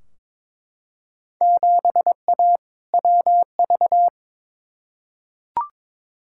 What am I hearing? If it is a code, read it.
Morse code, heard as a keyed tone, about 22 wpm, 700 Hz. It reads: 7A WV
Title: Field Day Exchanges Encore - 22wpm